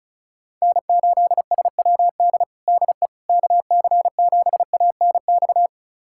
Morse code, heard as a keyed tone, about 34 wpm, 700 Hz. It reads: N8SWD DE KC7ANX
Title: USA Callsign DE Callsign 35wpm